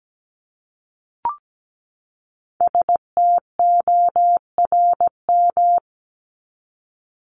Morse code, heard as a keyed tone, about 17 wpm, 700 Hz. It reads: STORM
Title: Top 1251-1500 Words 17wpm